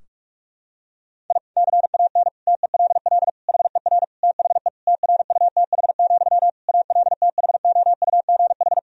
Morse code, heard as a keyed tone, about 46 wpm, 700 Hz. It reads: I CAN TELL HER THE TRUTH, ALTHOUGH
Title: Sentences from Top 1000 Words - Review Part 3 - 45wpm